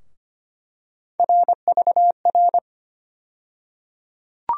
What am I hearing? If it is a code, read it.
Morse code, heard as a keyed tone, about 25 wpm, 700 Hz. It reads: RVR
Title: USA Call Sign Postfixes - 3 Letters 25wpm